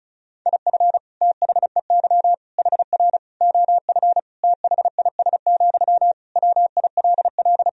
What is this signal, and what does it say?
Morse code, heard as a keyed tone, about 36 wpm, 700 Hz. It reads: IF THEY HR OF THIS, WILL
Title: Sentences from Top 400 Words and Abbreviations 35wpm